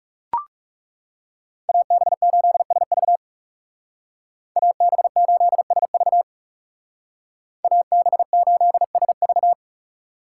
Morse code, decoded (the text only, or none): AB8SV AB8SV AB8SV